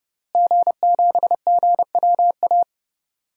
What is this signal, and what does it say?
Morse code, heard as a keyed tone, about 30 wpm, 700 Hz. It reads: G7GWA
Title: DXCC-100 Call Signs - 5 Characters - Repeated 3x - 30wpm